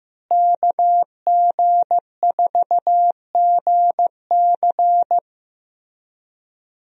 Morse code, heard as a keyed tone, about 15 wpm, 700 Hz. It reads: KG4GC
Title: USA Call Signs - 5 Characters - Repeated 3x - 15wpm